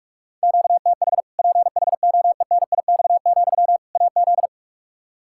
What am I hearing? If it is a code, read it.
Morse code, heard as a keyed tone, about 46 wpm, 700 Hz. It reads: QTH PHOENIX, AZ
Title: Practice QSO - Exchange 1 - Beginning - 45wpm